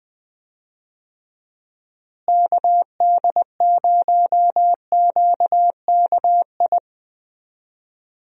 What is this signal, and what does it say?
Morse code, heard as a keyed tone, about 20 wpm, 700 Hz. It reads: KD0QKI
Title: USA Call Signs - 6 Characters - Repeated 2x - 20wpm